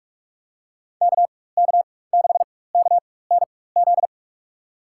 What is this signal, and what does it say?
Morse code, heard as a keyed tone, about 44 wpm, spaced as for 21 wpm, 700 Hz. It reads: KK6KNZ
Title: USA Call Signs - 4x Character Spacing - 45wpm